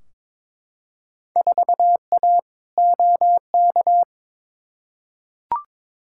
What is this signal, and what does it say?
Morse code, heard as a keyed tone, about 22 wpm, 700 Hz. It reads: 4A OK
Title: Field Day Exchanges - 22wpm